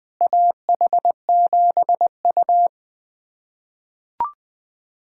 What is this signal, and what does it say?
Morse code, heard as a keyed tone, about 20 wpm, 700 Hz. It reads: AH7U
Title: USA Callsign DE Callsign - 4 Characters - 20wpm